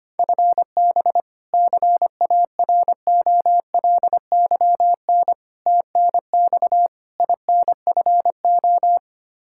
Morse code, heard as a keyed tone, about 25 wpm, 700 Hz. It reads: FB CAROLYN TNX INFO
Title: Practice QSO - Exchange 3 - More Info - 25wpm